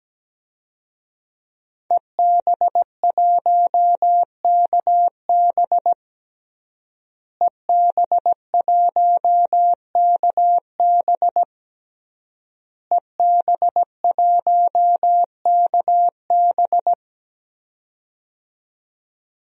Morse code, heard as a keyed tone, about 17 wpm, 700 Hz. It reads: EB1KB EB1KB EB1KB
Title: DXCC-100 Call Signs - 5 Characters - Repeated 3x - 17wpm